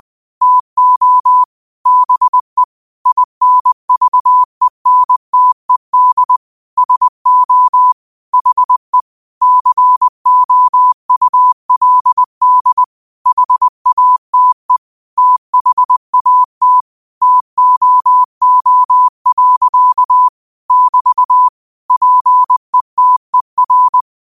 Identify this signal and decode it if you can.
Morse code, heard as a keyed tone, about 20 wpm, 1000 Hz. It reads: TO BE INVENTED SO HE COULD HATE THAT TOO. <BT> PETER
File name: qotdfiles / 20wpm